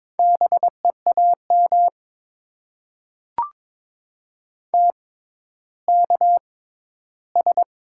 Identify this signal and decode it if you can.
Morse code, heard as a keyed tone, about 22 wpm, spaced as for 13 wpm, 700 Hz. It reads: BEAM TKS
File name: Top 90 QSO Elements - 6x Character Spacing - 22wpm